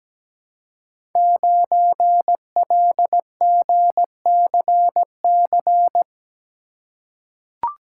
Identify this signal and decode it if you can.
Morse code, heard as a keyed tone, about 17 wpm, 700 Hz. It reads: 9LGCC
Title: Code Groups - 5 Letter:Numbers - 17wpm